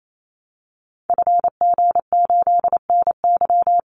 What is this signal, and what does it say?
Morse code, heard as a keyed tone, about 28 wpm, 700 Hz. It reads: FG8NY
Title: Call Signs Repeated 2x - 28wpm